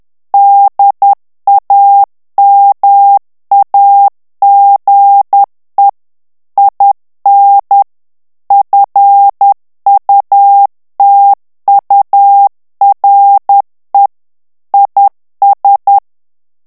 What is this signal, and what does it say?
Morse code, heard as a keyed tone, about 10 wpm, 800 Hz. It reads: DAMAGE IN FUTURE IS
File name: morse mp3 / ANGLAIS texte clair 2 vitesse 10 mots minute